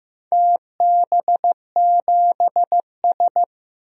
Morse code, heard as a keyed tone, about 15 wpm, 700 Hz. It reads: TB7S
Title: DXCC-100 Call Signs - 4 Characters - Repeated 3x - 15wpm